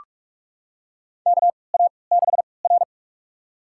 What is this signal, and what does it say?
Morse code, heard as a keyed tone, about 44 wpm, spaced as for 25 wpm, 690 Hz. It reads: KA6R
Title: USA Call Signs - 4 Characters - 3x Character Spacing - 45wpm